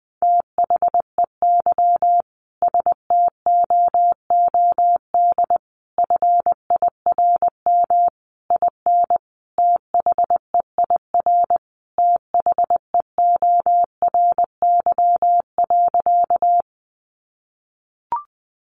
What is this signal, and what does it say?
Morse code, heard as a keyed tone, about 20 wpm, 700 Hz. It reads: THEY STOOD FIRM IN THEIR THEORY.
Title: Sentences from Top 1400 Words 20wpm